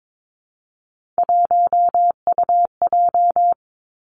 Morse code, heard as a keyed tone, about 22 wpm, 700 Hz. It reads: 1UJ
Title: Three Challenging Letter-Number Combinations 22wpm